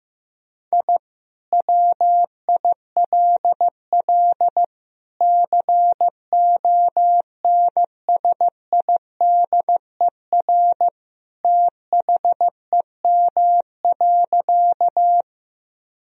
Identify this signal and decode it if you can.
Morse code, heard as a keyed tone, about 15 wpm, 700 Hz. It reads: I WILL CONSIDER THEM.
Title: Sentences from Top 1000 Words - Review Part 4 - 15wpm